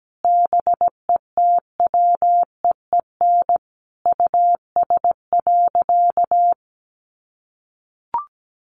Morse code, heard as a keyed tone, about 17 wpm, 700 Hz. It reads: BETWEEN US.
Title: Sentences from Top 500 Words 17wpm